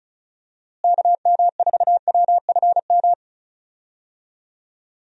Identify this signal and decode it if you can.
Morse code, heard as a keyed tone, about 36 wpm, 700 Hz. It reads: KM4WFM
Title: USA Call Signs - 6 Characters - Repeated 3x - 35wpm